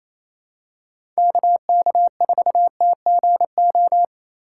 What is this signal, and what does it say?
Morse code, heard as a keyed tone, about 28 wpm, 700 Hz. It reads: KK4TGO
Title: USA Call Signs - 6 Characters - 3x Character Spacing - 28wpm